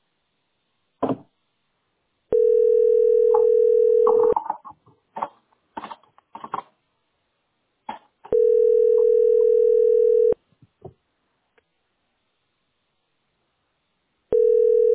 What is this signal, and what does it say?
Ringback tone (North America), 440 + 480 Hz.